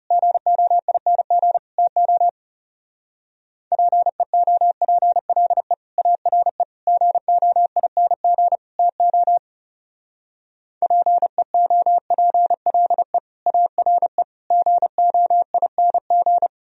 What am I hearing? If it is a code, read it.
Morse code, heard as a keyed tone, about 36 wpm, 700 Hz. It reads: GOING TO PEOPLE ARE GOING TO PEOPLE ARE GOING
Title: Speed-Racing - Sets of 4 Words 30wpm